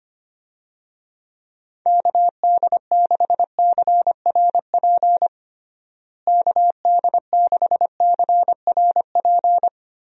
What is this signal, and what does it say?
Morse code, heard as a keyed tone, about 25 wpm, 700 Hz. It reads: KD6CRP KD6CRP
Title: USA Call Signs - 6 Characters - Repeated 3x - 25wpm